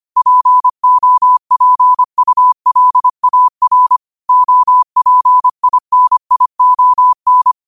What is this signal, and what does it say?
Morse code, heard as a keyed tone, about 25 wpm, 1000 Hz. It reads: POPULAR OPINION
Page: Quotes for Wed, 13 Aug 2025 in Morse Code at 25 words per minute.